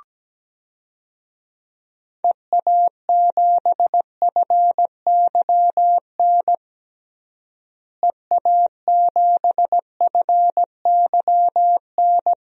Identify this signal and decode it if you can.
Morse code, heard as a keyed tone, about 17 wpm, 700 Hz. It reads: EA7FYN EA7FYN
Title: DXCC-100 Call Signs - 6 Characters - Repeated 2x - 17wpm